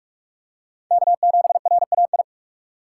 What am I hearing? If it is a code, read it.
Morse code, heard as a keyed tone, about 44 wpm, 700 Hz. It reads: K7RAI